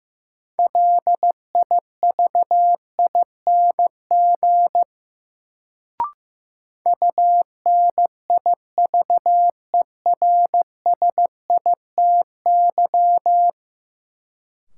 Morse code, heard as a keyed tone, about 15 wpm, 700 Hz. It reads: LIVING UNIVERSITY
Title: Sets of 4 Words - New Word Drill - 15wpm